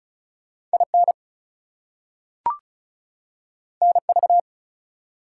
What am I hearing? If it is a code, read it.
Morse code, heard as a keyed tone, about 36 wpm, 700 Hz. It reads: IN NV
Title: US States and Canadian Provinces 35wpm